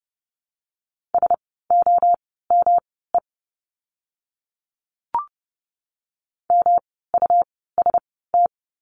Morse code, heard as a keyed tone, about 30 wpm, spaced as for 17 wpm, 700 Hz. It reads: SOME MUST
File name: course / 30:10wpm / 237 Lesson 38f - Letter q - Words 30wpm